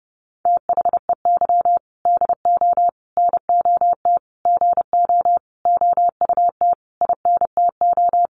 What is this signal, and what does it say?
Morse code, heard as a keyed tone, about 30 wpm, 700 Hz. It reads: THEY DO NOT GO OUT INTO
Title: Sentences from Top 700 Words 30wpm